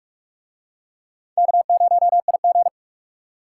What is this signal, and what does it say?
Morse code, heard as a keyed tone, about 44 wpm, 700 Hz. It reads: K0IG